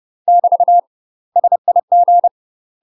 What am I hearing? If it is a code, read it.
Morse code, heard as a keyed tone, about 30 wpm, 700 Hz. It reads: <BT> SIG
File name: UK MASTER QSOs TRAINING / 30WPM / UK MASTER QSOs 0003